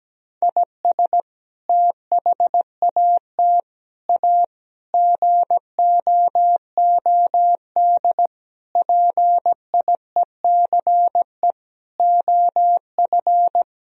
Morse code, heard as a keyed tone, about 17 wpm, 700 Hz. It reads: IS THAT A GOOD PIECE OF
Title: Sentences from Top 1000 Words - Review Part 3 - 17wpm